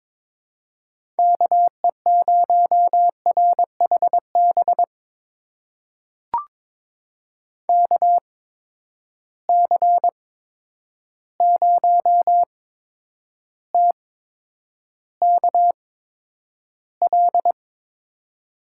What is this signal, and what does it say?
Morse code, heard as a keyed tone, about 22 wpm, spaced as for 9 wpm, 700 Hz. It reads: KE0RHB KC0TKL
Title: USA Call Signs - 6 Characters - 8x Character Spacing - 22wpm